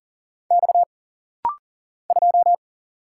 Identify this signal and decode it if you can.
Morse code, heard as a keyed tone, about 40 wpm, 700 Hz. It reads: X 2
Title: Single Letter-Number Alt - Rapid-Fire 40wpm